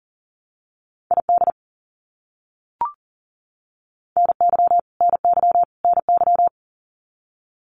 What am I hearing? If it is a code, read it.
Morse code, heard as a keyed tone, about 39 wpm, 700 Hz. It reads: ID NY NY NY
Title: US States Repeated 3x 40wpm